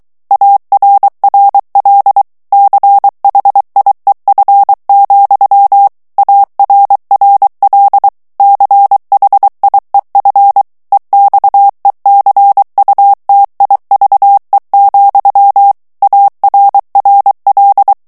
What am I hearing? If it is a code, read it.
Morse code, heard as a keyed tone, about 23 wpm, 800 Hz. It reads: ARRL CHIEF, ARRL CHIEF EXECUTIVE, ARRL
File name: ANGLAIS texte clair 3 vitesse 22 mots minute